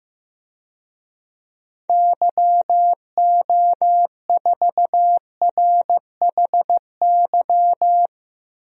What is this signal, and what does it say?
Morse code, heard as a keyed tone, about 15 wpm, 700 Hz. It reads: YO4RHY
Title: DXCC-100 Call Signs - 6 Characters - 15wpm